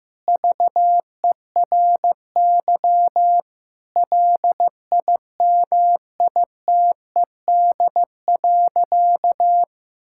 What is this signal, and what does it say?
Morse code, heard as a keyed tone, about 16 wpm, 700 Hz. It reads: VERY LIMITED.
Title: Sentences from Top 1000 Words - Review Part 6 - 15wpm